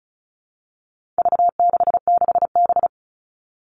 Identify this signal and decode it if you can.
Morse code, heard as a keyed tone, about 34 wpm, 700 Hz. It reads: V66B
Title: All Cut Numbers - 4-Digit 35wpm